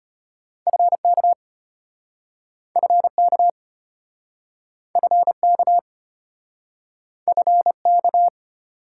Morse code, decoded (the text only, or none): FK FK FK FK